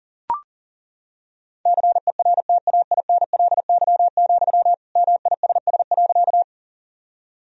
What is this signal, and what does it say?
Morse code, heard as a keyed tone, about 40 wpm, 700 Hz. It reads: CERTAINLY, MISS.
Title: Sentences from Top 1000 Words - Review Part 3 - 40wpm